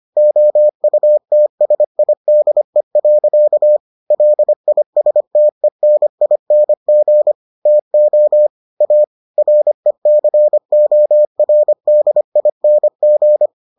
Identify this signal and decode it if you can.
Morse code, heard as a keyed tone, about 25 wpm, 600 Hz. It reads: OUTSIDE. LISTENING TO A RECORDING